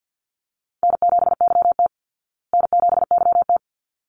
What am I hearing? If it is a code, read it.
Morse code, heard as a keyed tone, about 50 wpm, 700 Hz. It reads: N7YT N7YT
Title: Call Signs Repeated 3x - 50wpm